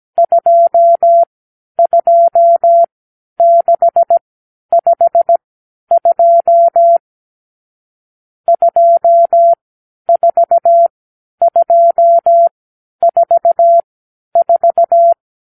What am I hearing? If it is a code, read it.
Morse code, heard as a keyed tone, about 17 wpm, spaced as for 10 wpm, 680 Hz. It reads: 22652 24244
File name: k7qo / Lesson 52 (8)